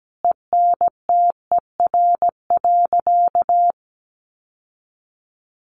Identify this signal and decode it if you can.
Morse code, heard as a keyed tone, about 17 wpm, 700 Hz. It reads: ENTER.
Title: Sentences from Top 1300 Words 17wpm